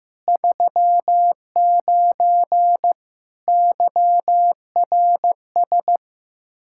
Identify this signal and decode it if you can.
Morse code, heard as a keyed tone, about 15 wpm, 700 Hz. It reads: 39 YRS
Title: Practice QSO - Exchange 3 - More Info - 15wpm